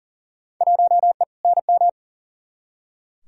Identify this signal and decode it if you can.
Morse code, heard as a keyed tone, about 40 wpm, 700 Hz. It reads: 1E NM
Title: Field Day Exchanges - 40wpm